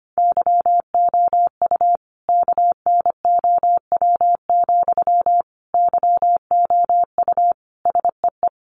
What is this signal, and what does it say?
Morse code, heard as a keyed tone, about 25 wpm, 700 Hz. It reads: YOU KNOW, YOU SEE
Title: Sentences from Top 200 Words 25wpm